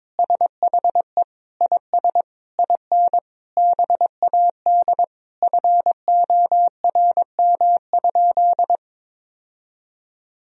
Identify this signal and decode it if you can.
Morse code, heard as a keyed tone, about 21 wpm, 700 Hz. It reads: SHE IS IN BAD FORM?
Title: Sentences from Top 1000 Words - Review Part 1 - 22wpm